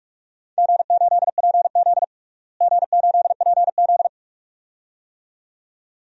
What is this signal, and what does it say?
Morse code, heard as a keyed tone, about 46 wpm, 700 Hz. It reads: G8PZ G8PZ
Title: DXCC-100 Call Signs - 4 Characters - Repeated 2x - 45wpm